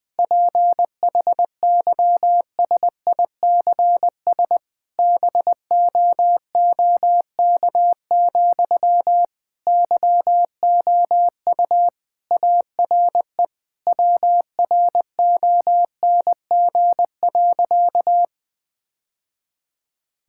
Morse code, heard as a keyed tone, about 20 wpm, 700 Hz. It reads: PHYSICS BOOK, YOU ARE WRONG.
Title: Sentences from Top 1000 Words - Review Part 5 - 20wpm